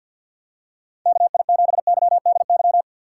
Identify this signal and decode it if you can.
Morse code, heard as a keyed tone, about 48 wpm, 700 Hz. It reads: KI7YDY
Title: USA Call Signs - 6 Characters - 3x Character Spacing - 50wpm